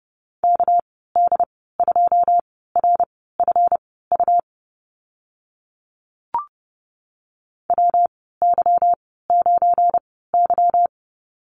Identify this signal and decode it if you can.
Morse code, heard as a keyed tone, about 30 wpm, spaced as for 17 wpm, 700 Hz. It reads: KD2RFU WY9Y
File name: Lesson 37g - Letter x - Call Signs 30wpm